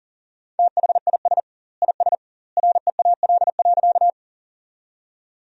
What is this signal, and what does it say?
Morse code, heard as a keyed tone, about 40 wpm, 700 Hz. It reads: THIS IS REAL.
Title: 2x Word-Spacing - Sentences from Top 300 Words - 40wpm